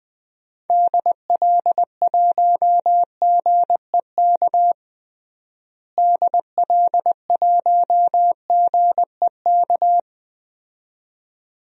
Morse code, heard as a keyed tone, about 21 wpm, 700 Hz. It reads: DL1GEK DL1GEK
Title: DXCC-100 Call Signs - 6 Characters - Repeated 2x - 20wpm